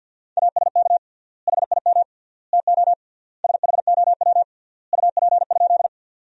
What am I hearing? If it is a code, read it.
Morse code, heard as a keyed tone, about 50 wpm, 700 Hz. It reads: ASK HIM TO SHOW UP?
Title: 3x Word-Spacing - Sentences from Top 200 Words - 50wpm